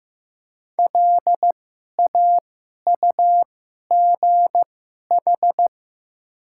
Morse code, heard as a keyed tone, about 15 wpm, spaced as for 11 wpm, 700 Hz. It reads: LAUGH
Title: 2x Character Spacing - Top 500 Words - 15wpm